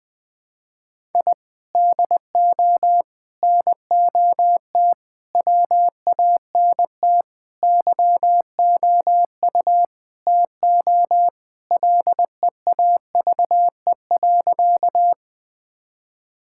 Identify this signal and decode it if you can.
Morse code, heard as a keyed tone, about 20 wpm, 700 Hz. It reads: I DO NOT WANT YOU TO LEAVE.
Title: Sentences with 5-Word Phrases - Spoken Beginning - 20wpm